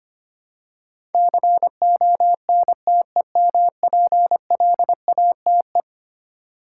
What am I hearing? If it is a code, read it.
Morse code, heard as a keyed tone, about 25 wpm, 700 Hz. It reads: CONTEMPLATE
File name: practice / 25wpm / Eleven Letter Words - 25wpm